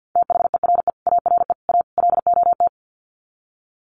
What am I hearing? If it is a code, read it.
Morse code, heard as a keyed tone, about 50 wpm, 700 Hz. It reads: THERE ARE A LOT
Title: Sets of 4 Words 50wpm